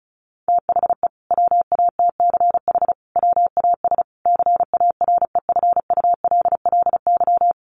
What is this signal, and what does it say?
Morse code, heard as a keyed tone, about 34 wpm, 700 Hz. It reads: THE WATCH WAS CAREFULLY
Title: Sentences from Top 1300 Words 35wpm